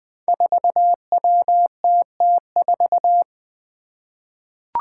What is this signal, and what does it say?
Morse code, heard as a keyed tone, about 20 wpm, 700 Hz. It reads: 4WTT4
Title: Five Challenging Letter-Number Combinations 20wpm